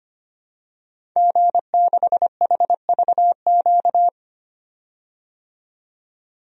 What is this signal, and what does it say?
Morse code, heard as a keyed tone, about 25 wpm, 700 Hz. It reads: G6HVQ
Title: DXCC-100 Call Signs - 5 Characters - 25wpm